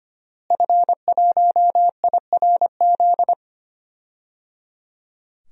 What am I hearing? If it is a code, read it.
Morse code, heard as a keyed tone, about 25 wpm, 700 Hz. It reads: F1IRZ
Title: DXCC-100 Call Signs - 5 Characters - Repeated 3x - 25wpm